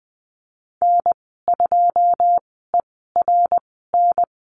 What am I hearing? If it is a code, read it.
Morse code, heard as a keyed tone, about 20 wpm, spaced as for 15 wpm, 700 Hz. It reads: N2ERN
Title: 082 Lesson 14g - Number 2 - Call Signs 20wpm